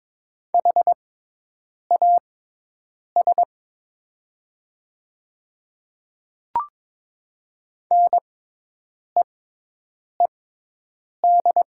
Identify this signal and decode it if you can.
Morse code, heard as a keyed tone, about 22 wpm, spaced as for 8 wpm, 700 Hz. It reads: HAS NEED
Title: 6x Character Spacing - Top 200 Words - 22wpm